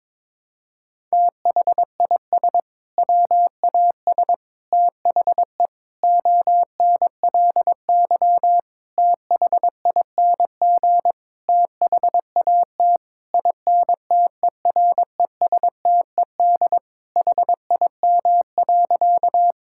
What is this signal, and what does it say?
Morse code, heard as a keyed tone, about 22 wpm, 700 Hz. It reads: THIS WAS THE ONLY THING THAT INTERESTED HIM.
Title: Sentences from Top 1000 Words - Review Part 6 - 22wpm